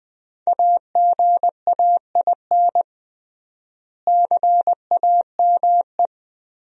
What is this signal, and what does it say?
Morse code, heard as a keyed tone, about 20 wpm, 700 Hz. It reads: AGAIN CAME
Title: 3x Word-Spacing - Sentences from Top 500 Words - 20wpm